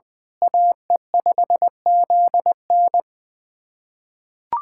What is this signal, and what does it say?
Morse code, heard as a keyed tone, about 20 wpm, 700 Hz. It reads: AE5ZN
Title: USA Callsign DE Callsign 20wpm